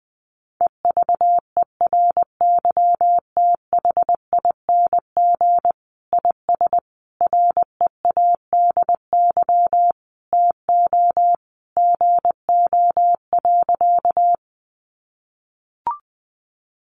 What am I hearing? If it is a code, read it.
Morse code, heard as a keyed tone, about 20 wpm, 700 Hz. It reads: EVERYTHING IS READY TO GO.
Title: Sentences from Top 700 Words 20wpm